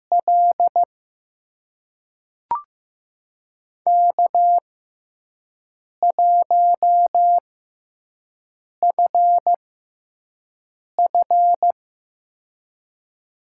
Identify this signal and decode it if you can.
Morse code, heard as a keyed tone, about 15 wpm, spaced as for 5 wpm, 700 Hz. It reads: L K1FF
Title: USA Call Signs - 4 Characters - 6x Character Spacing - 15wpm